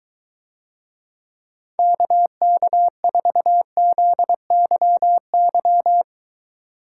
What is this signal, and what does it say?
Morse code, heard as a keyed tone, about 24 wpm, 700 Hz. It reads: KK4ZYY